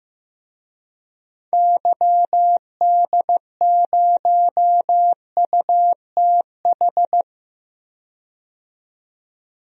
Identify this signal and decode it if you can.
Morse code, heard as a keyed tone, about 15 wpm, 700 Hz. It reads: YD0UTH
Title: DXCC-100 Call Signs - 6 Characters - 15wpm